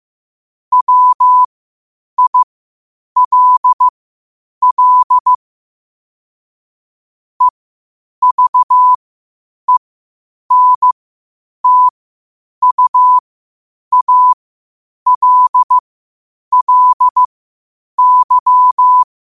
Morse code, heard as a keyed tone, about 15 wpm, spaced as for 8 wpm, 1000 Hz. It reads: WILL EVENTUALLY
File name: qotdfiles / 8wpm